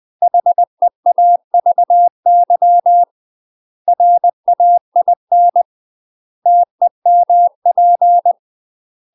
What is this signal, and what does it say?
Morse code, heard as a keyed tone, about 20 wpm, 700 Hz. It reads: HEAVY RAIN TEMP
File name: UK MASTER QSOs TRAINING / 20WPM / UK MASTER QSOs 0004